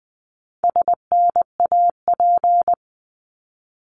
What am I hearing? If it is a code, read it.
Morse code, heard as a keyed tone, about 20 wpm, 700 Hz. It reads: SNAP
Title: 260 Lesson 42e - Words 20-20wpm